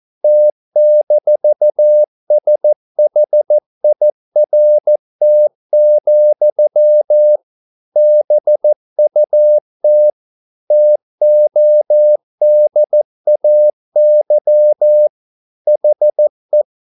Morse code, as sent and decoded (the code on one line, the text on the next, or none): - -....- ... .... .. .-. - --..-- / -... ..- - / - --- -.. .- -.-- / .... .
T-SHIRT, BUT TODAY HE